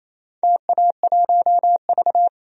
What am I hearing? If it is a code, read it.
Morse code, heard as a keyed tone, about 28 wpm, 700 Hz. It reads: TA1V
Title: DXCC-100 Call Signs - 4 Characters - Repeated 3x - 28wpm